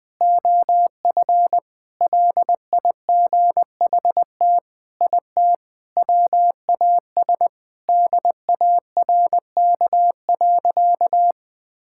Morse code, heard as a keyed tone, about 20 wpm, 700 Hz. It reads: OF LIGHT IT WAS DARK.
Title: Sentences with 4-Word Phrases - 20wpm